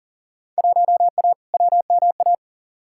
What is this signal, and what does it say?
Morse code, heard as a keyed tone, about 40 wpm, 700 Hz. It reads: 1A WMA
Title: Field Day Exchanges - 40wpm